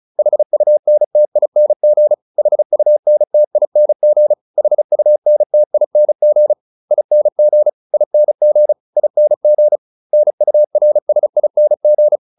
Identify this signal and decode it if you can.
Morse code, heard as a keyed tone, about 34 wpm, 600 Hz. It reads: HUNTING HUNTING HUNTING ING ING ING NURSING
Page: advanced-practice-files